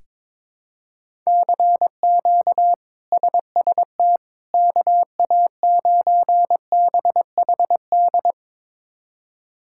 Morse code, heard as a keyed tone, about 22 wpm, 700 Hz. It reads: CQ SST KA9BHD
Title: K1USN SST QSO Practice - 22wpm